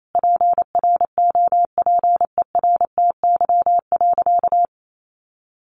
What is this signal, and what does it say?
Morse code, heard as a keyed tone, about 28 wpm, 700 Hz. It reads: PROPERTY.
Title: Sentences from Top 1000 Words 28wpm